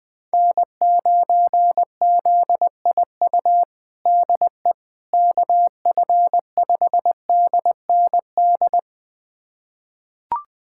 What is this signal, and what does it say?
Morse code, heard as a keyed tone, about 20 wpm, 700 Hz. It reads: N9ZIU DE KF5DND
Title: USA Callsign DE Callsign 20wpm